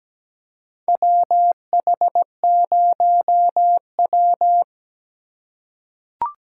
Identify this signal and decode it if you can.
Morse code, heard as a keyed tone, about 17 wpm, 700 Hz. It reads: WH0W
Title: USA Call Signs - 4 Characters - 2x Character Spacing - 17wpm